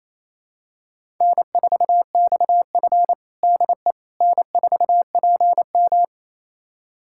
Morse code, heard as a keyed tone, about 28 wpm, 700 Hz. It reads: N4XF DE N4PM
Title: USA Callsign DE Callsign - 4 Characters - 28wpm